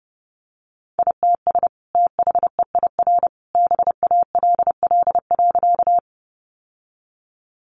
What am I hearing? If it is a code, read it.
Morse code, heard as a keyed tone, about 28 wpm, 700 Hz. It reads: ITS THEIR BALL.
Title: Sentences from Top 900 Words 30wpm